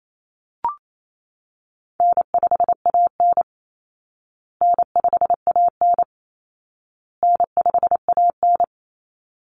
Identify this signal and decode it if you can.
Morse code, heard as a keyed tone, about 28 wpm, 700 Hz. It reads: N5AN N5AN N5AN
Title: Easy Call Signs Repeated 3x - 28wpm